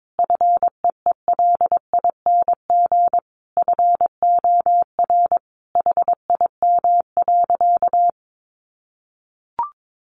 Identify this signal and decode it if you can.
Morse code, heard as a keyed tone, about 22 wpm, 700 Hz. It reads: FEELING FOR HIM.
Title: Sentences from Top 1200 Words 22wpm